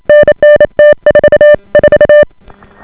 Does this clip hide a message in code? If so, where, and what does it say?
Yes: Morse code from 0.1 to 2.2 s, heard as a keyed tone, about 28 wpm, 590 Hz. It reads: NN T44